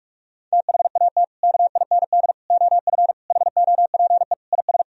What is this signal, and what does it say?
Morse code, heard as a keyed tone, about 46 wpm, 700 Hz. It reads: THAT KIND OF HOPE IS
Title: Sentences from Top 1000 Words - Review Part 3 - 45wpm